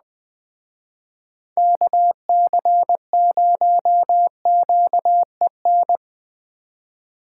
Morse code, heard as a keyed tone, about 20 wpm, 700 Hz. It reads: KC0QEN